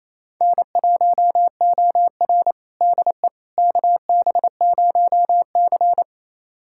Morse code, heard as a keyed tone, about 28 wpm, 700 Hz. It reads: N1OR DE KB0C
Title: USA Callsign DE Callsign - 4 Characters - 28wpm